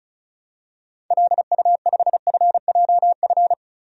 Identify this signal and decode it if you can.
Morse code, heard as a keyed tone, about 35 wpm, 700 Hz. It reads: LU5FJF
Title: DXCC-100 Call Signs - 6 Characters - 35wpm